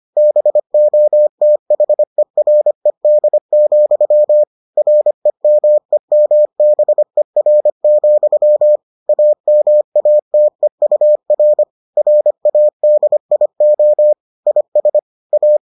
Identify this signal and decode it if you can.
Morse code, heard as a keyed tone, about 25 wpm, 600 Hz. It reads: BOTHERED, REMEMBER, AMATEUR RADIO IS A